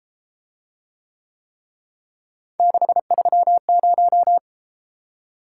Morse code, heard as a keyed tone, about 33 wpm, 700 Hz. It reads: 630